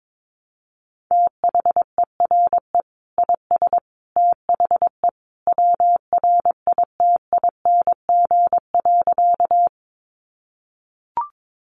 Morse code, heard as a keyed tone, about 22 wpm, 700 Hz. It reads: THERE IS THE WRITING.
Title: Sentences from Top 800 Words 22wpm